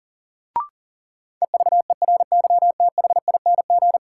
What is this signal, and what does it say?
Morse code, heard as a keyed tone, about 40 wpm, 700 Hz. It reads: EVERYTHING
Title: Sentences from Top 700 Words - New Word Drill - 40wpm